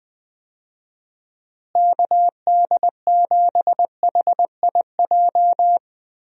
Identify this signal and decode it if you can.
Morse code, heard as a keyed tone, about 21 wpm, 700 Hz. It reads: KD7HIJ
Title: USA Call Signs - 6 Characters - Repeated 2x - 20wpm